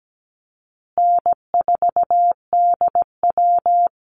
Morse code, heard as a keyed tone, about 17 wpm, 700 Hz. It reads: N4DW
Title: Easy Call Signs Repeated 3x - 17wpm